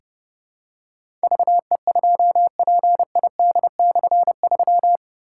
Morse code, heard as a keyed tone, about 30 wpm, 700 Hz. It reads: VE2PID/3
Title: K1USN SST QSO Practice - 30wpm